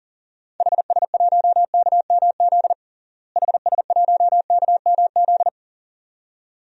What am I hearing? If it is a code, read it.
Morse code, heard as a keyed tone, about 40 wpm, 700 Hz. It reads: HS1KMZ HS1KMZ
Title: DXCC-100 Call Signs - 6 Characters - Repeated 2x - 40wpm